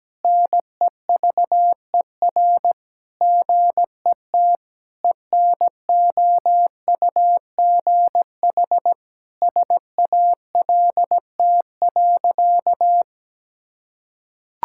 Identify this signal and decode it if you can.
Morse code, heard as a keyed tone, about 17 wpm, 700 Hz. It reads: NEVER GET ENOUGH SALT.
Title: Sentences from Top 1000 Words - Review Part 5 - 17wpm